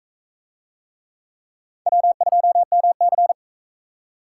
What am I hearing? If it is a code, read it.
Morse code, heard as a keyed tone, about 41 wpm, 700 Hz. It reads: W2MC